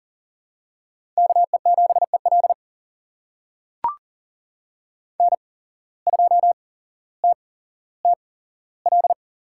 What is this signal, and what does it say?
Morse code, heard as a keyed tone, about 40 wpm, spaced as for 17 wpm, 700 Hz. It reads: KE7EL N2TTL
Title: USA Call Signs - 5 Characters - 8x Character Spacing - 40wpm